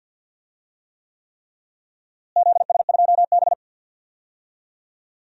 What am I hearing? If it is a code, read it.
Morse code, heard as a keyed tone, about 50 wpm, 700 Hz. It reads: ZS2B
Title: DXCC-100 Call Signs - 4 Characters - Repeated 3x - 50wpm